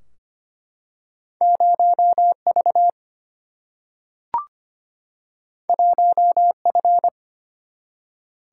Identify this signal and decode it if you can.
Morse code, heard as a keyed tone, about 25 wpm, 700 Hz. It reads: 0V 1F
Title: Single-Letter Single-Number 25wpm